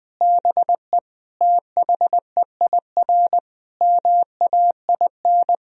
Morse code, heard as a keyed tone, about 20 wpm, 700 Hz. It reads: BE THEIR MAIN
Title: Sentences from Top 1000 Words - Review Part 2 - 20wpm